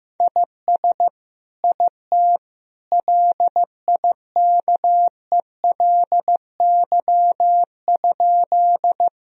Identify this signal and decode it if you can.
Morse code, heard as a keyed tone, about 15 wpm, 700 Hz. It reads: IS IT LIKELY?
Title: Sentences from Top 1000 Words - Review Part 4 - 15wpm